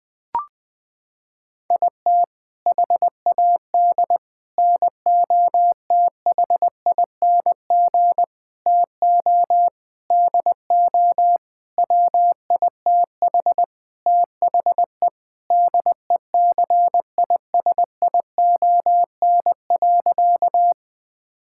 Morse code, heard as a keyed tone, about 20 wpm, 700 Hz. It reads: IT HAD NOTHING TO DO WITH THE DECISION.
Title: Sentences with 6-Word Phrases - 20wpm